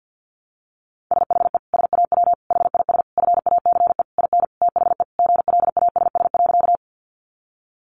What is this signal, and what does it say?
Morse code, heard as a keyed tone, about 50 wpm, 700 Hz. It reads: SHE SAW HIS FACE IN THE GLASS.
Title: Sentences from Top 1300 Words 50wpm